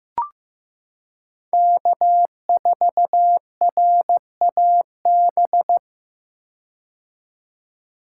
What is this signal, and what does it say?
Morse code, heard as a keyed tone, about 15 wpm, 700 Hz. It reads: K4RAB
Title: K1USN SST QSO Practice - 15wpm